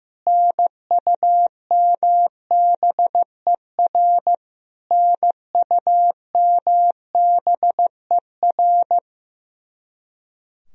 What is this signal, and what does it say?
Morse code, heard as a keyed tone, about 15 wpm, 700 Hz. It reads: NUMBER NUMBER
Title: Top 100 Words Repeated 15wpm